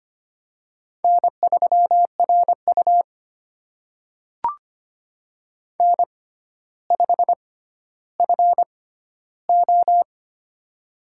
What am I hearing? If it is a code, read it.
Morse code, heard as a keyed tone, about 25 wpm, spaced as for 13 wpm, 700 Hz. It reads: N3RU N5FO
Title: USA Call Signs - 4 Characters - 6x Character Spacing - 25wpm